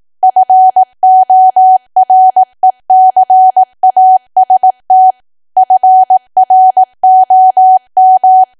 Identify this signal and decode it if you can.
Morse code, heard as a keyed tone, about 18 wpm, 750 Hz. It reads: FORECAST FROM
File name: cwa / W1AW-Level2 / 18WPM